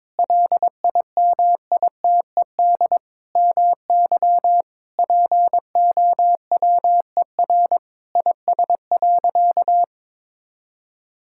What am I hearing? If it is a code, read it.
Morse code, heard as a keyed tone, about 22 wpm, 700 Hz. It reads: LIMITED MY POWER IS.
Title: Sentences from Top 1000 Words - Review Part 6 - 22wpm